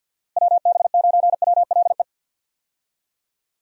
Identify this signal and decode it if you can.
Morse code, heard as a keyed tone, about 50 wpm, 700 Hz. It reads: WB9WLE
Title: USA Call Signs - 6 Characters - Repeated 3x - 50wpm